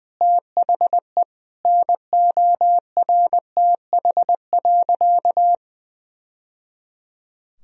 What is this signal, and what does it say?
Morse code, heard as a keyed tone, about 20 wpm, 700 Hz. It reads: THE NORTH.
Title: Sentences from Top 1000 Words - Review Part 1 - 20wpm